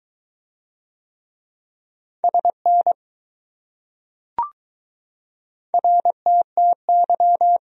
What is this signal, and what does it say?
Morse code, heard as a keyed tone, about 19 wpm, 700 Hz. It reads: SN RTTY